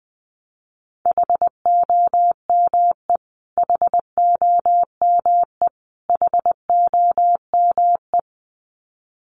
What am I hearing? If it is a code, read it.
Morse code, heard as a keyed tone, about 20 wpm, 700 Hz. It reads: HOME HOME HOME
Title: Top 300 Words Repeated 20wpm